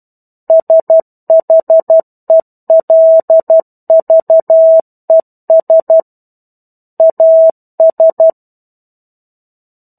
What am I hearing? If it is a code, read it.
Morse code, heard as a keyed tone, about 12 wpm, 650 Hz. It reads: SHELVES AS
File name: 01-Chapter-01 12 WPM